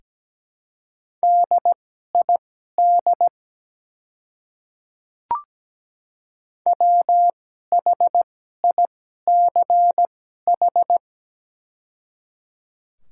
Morse code, heard as a keyed tone, about 17 wpm, spaced as for 12 wpm, 700 Hz. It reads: DID WHICH
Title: 2x Character Spacing - Top 200 Words - 17wpm